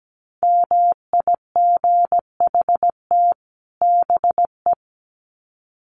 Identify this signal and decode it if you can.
Morse code, heard as a keyed tone, about 17 wpm, 700 Hz. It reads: MIGHT BE
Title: Sets of 2 Words 17wpm